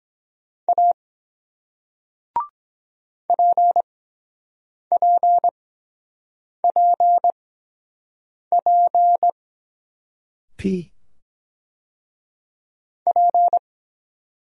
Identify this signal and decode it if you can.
Morse code, heard as a keyed tone, about 24 wpm, spaced as for 7 wpm, 700 Hz. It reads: A PPPP P